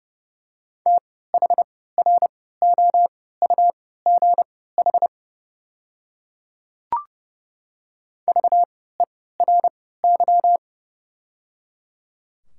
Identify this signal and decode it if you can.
Morse code, heard as a keyed tone, about 30 wpm, spaced as for 17 wpm, 700 Hz. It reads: THROUGH VERY
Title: 3x Character Spacing - Top 200 Words - 30wpm